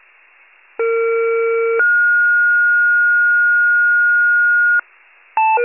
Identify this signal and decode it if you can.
Two-tone page, heard as a one-second tone, then a three-second tone, about 460 then 1430 Hz. It is heard from 0.8 to 4.8 s.